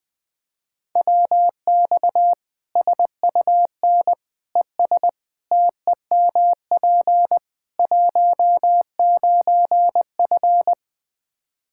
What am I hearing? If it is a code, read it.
Morse code, heard as a keyed tone, about 20 wpm, 700 Hz. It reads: WX SUN ES TEMP 19F
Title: Practice QSO - Exchange 2 - Info - 20wpm